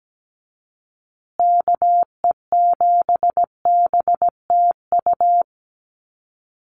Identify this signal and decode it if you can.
Morse code, heard as a keyed tone, about 17 wpm, 700 Hz. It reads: KE7BTU
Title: USA Call Signs Repeated 2x 17wpm